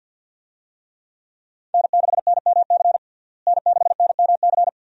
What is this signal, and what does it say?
Morse code, heard as a keyed tone, about 47 wpm, 700 Hz. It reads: N6NMC N6NMC
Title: USA Call Signs - 5 Characters - Repeated 2x - 50wpm